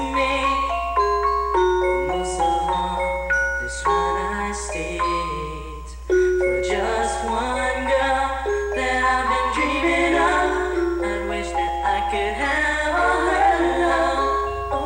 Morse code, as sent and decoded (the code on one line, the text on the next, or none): none